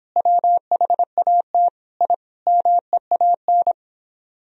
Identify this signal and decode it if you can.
Morse code, heard as a keyed tone, about 29 wpm, 700 Hz. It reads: WHAT I MEAN